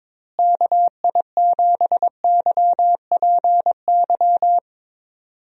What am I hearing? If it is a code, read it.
Morse code, heard as a keyed tone, about 22 wpm, 700 Hz. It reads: KI7YPY
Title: USA Call Signs - 6 Characters - Repeated 2x - 22wpm